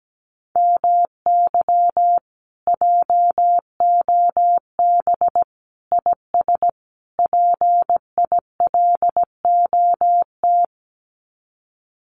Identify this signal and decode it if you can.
Morse code, heard as a keyed tone, about 17 wpm, 700 Hz. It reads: MY JOB IS PILOT
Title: Common QSO Phrases Encore 17wpm